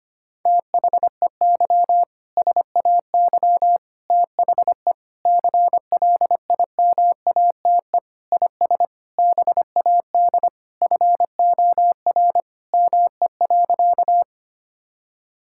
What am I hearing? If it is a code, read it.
Morse code, heard as a keyed tone, about 25 wpm, 700 Hz. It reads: THEY SAY THE CLIMATE IS BAD FOR ME.
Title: Sentences from Top 1600 Words 25wpm